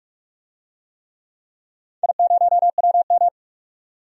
Morse code, heard as a keyed tone, about 46 wpm, 700 Hz. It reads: I0WM
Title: DXCC-100 Call Signs - 4 Characters - 45wpm